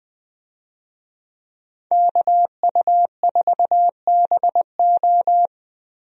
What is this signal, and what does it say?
Morse code, heard as a keyed tone, about 20 wpm, 700 Hz. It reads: KU4BO
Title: USA Call Signs - 5 Characters - 20wpm